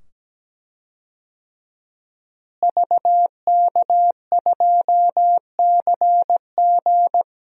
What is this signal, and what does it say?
Morse code, heard as a keyed tone, about 17 wpm, 700 Hz. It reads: VK2CG
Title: DXCC-100 Call Signs - 5 Characters - Repeated 2x - 17wpm